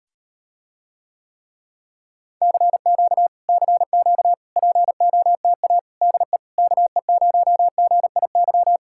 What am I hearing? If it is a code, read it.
Morse code, heard as a keyed tone, about 38 wpm, 700 Hz. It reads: CQ CQ POTA DE KE0GIY